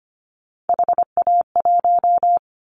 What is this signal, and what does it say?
Morse code, heard as a keyed tone, about 25 wpm, 700 Hz. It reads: HA1
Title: Callsign Prefixes for Top 25 DXCC - Repeated 3x - 25wpm